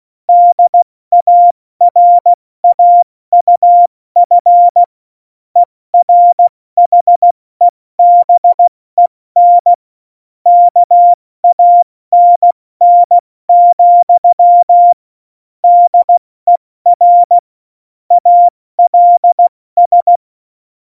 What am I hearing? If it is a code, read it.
Morse code, heard as a keyed tone, about 16 wpm, 700 Hz. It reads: DARAUF ERHEBEN KANN, DER ALS
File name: Morsekurs / Witze / Witz 13. Oktober 2020, 70 F80